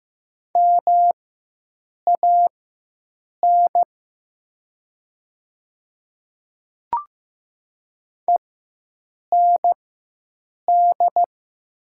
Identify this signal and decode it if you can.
Morse code, heard as a keyed tone, about 15 wpm, spaced as for 7 wpm, 700 Hz. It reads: MAN END
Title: 4x Character Spacing - Top 300 Words - 15wpm